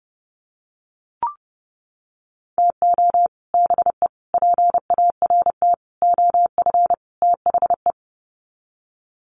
Morse code, heard as a keyed tone, about 30 wpm, 700 Hz. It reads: TO BE PART OF THE
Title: Sets of 5 Words 30wpm